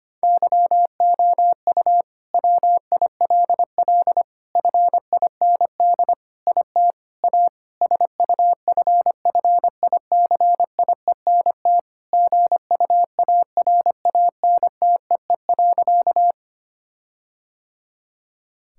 Morse code, heard as a keyed tone, about 25 wpm, 700 Hz. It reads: YOU WILL FIND IT A SUFFICIENT GUARANTEE.
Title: Sentences from Top 1600 Words 25wpm